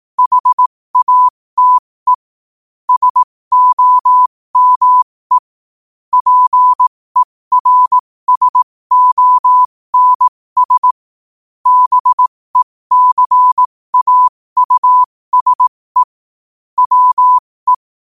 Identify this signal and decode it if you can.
Morse code, heard as a keyed tone, about 18 wpm, spaced as for 15 wpm, 1000 Hz. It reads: HATE SOME PERSONS BECAUSE WE
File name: qotdfiles / 15wpm